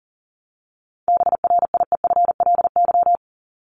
Morse code, heard as a keyed tone, about 40 wpm, 700 Hz. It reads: BRIEFLY
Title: LY Suffix - Top 100 Words with Base Words 40wpm